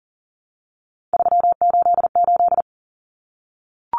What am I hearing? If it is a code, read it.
Morse code, heard as a keyed tone, about 40 wpm, 700 Hz. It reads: 388
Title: Common Cut Numbers - 3-Digit 40wpm